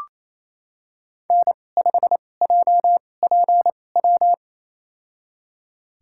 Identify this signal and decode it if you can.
Morse code, heard as a keyed tone, about 27 wpm, spaced as for 20 wpm, 700 Hz. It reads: N5JPW
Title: USA Call Signs - 5 Characters - 2x Character Spacing - 28wpm